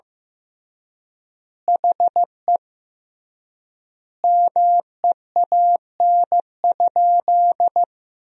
Morse code, heard as a keyed tone, about 15 wpm, 700 Hz. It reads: HE MEAN?
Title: 3x Word-Spacing - Sentences from Top 200 Words - 15wpm